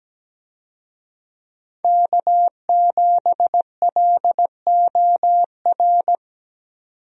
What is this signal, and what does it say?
Morse code, heard as a keyed tone, about 17 wpm, 700 Hz. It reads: K7LOR
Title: USA Call Signs - 5 Characters - 17wpm